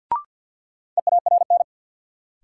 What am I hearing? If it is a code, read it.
Morse code, heard as a keyed tone, about 49 wpm, 700 Hz. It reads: EARN
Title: Sentences from Top 1500 Words - New Word Drill - 50wpm